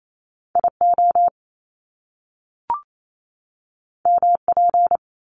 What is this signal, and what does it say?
Morse code, heard as a keyed tone, about 28 wpm, 700 Hz. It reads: IO MP
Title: Top Two Letter Combinations in Words 28wpm